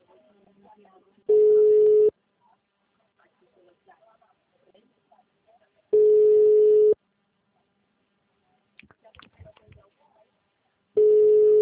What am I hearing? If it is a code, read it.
Ringback tone (Europe, 425 Hz), with speech.
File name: out- -2005- - - .84266